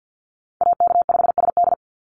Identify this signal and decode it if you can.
Morse code, heard as a keyed tone, about 50 wpm, 700 Hz. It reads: AK5SD
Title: NA SOTA Chaser Callsigns 50wpm